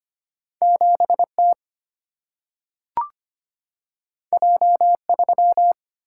Morse code, heard as a keyed tone, about 25 wpm, 700 Hz. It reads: 7T J3
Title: Single-Letter Single-Number 25wpm